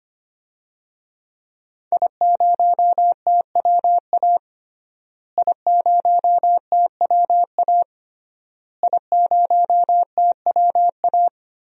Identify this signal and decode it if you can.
Morse code, heard as a keyed tone, about 25 wpm, 700 Hz. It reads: I0TWA I0TWA I0TWA
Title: DXCC-100 Call Signs - 5 Characters - Repeated 3x - 25wpm